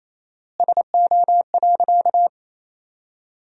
Morse code, heard as a keyed tone, about 28 wpm, 700 Hz. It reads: SO.
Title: 4x Word-Spacing - Sentences from Top 500 Words - 28wpm